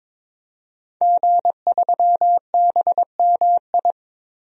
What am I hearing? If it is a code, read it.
Morse code, heard as a keyed tone, about 22 wpm, 700 Hz. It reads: G3BMI
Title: DXCC-100 Call Signs - 5 Characters - Repeated 2x - 22wpm